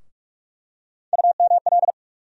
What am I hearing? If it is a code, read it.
Morse code, heard as a keyed tone, about 46 wpm, 700 Hz. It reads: UML
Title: USA Call Sign Postfixes - 3 Letters 45wpm